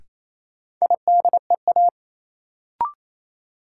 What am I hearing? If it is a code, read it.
Morse code, heard as a keyed tone, about 28 wpm, 700 Hz. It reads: IDEA
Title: Sentences from Top 300 Words - New Word Drill - 28wpm